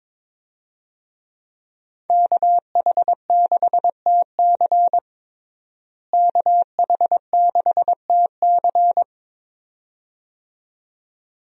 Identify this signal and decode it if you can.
Morse code, heard as a keyed tone, about 22 wpm, 700 Hz. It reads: KH6TC KH6TC
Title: DXCC-100 Call Signs - 5 Characters - Repeated 2x - 22wpm